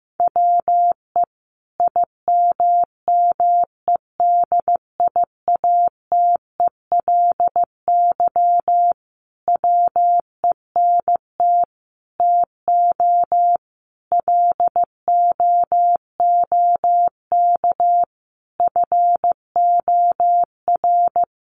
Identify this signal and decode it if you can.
Morse code, heard as a keyed tone, about 15 wpm, 700 Hz. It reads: WE IMMEDIATELY WENT TO LOOK FOR
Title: Sentences from Top 1100 Words 15wpm